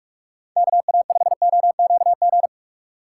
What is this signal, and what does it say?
Morse code, heard as a keyed tone, about 44 wpm, 700 Hz. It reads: KA5OQG